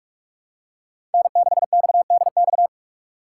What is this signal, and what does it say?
Morse code, heard as a keyed tone, about 46 wpm, 700 Hz. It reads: N6XDX
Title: USA Call Signs - 5 Characters - 6x Character Spacing - 45wpm